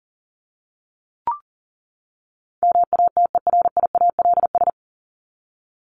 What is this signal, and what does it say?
Morse code, heard as a keyed tone, about 40 wpm, 700 Hz. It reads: MATERIALS
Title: ES:S Suffix - Top 100 Words with Base Words 40wpm